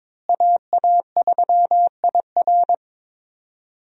Morse code, heard as a keyed tone, about 22 wpm, 700 Hz. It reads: AA3IR
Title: USA Call Signs - 5 Characters - Repeated 3x - 22wpm